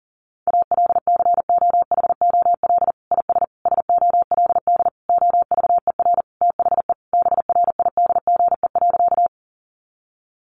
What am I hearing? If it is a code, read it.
Morse code, heard as a keyed tone, about 40 wpm, 700 Hz. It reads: ALCOHOL IS SOLD OVER THE BRIDGE.
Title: Sentences from Top 1400 Words 40wpm